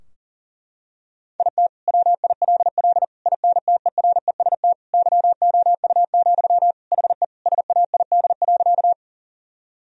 Morse code, heard as a keyed tone, about 40 wpm, 700 Hz. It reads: IT WILL INTEREST YOU, HE SAID.
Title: Sentences from Top 1000 Words - Review Part 2 - 40wpm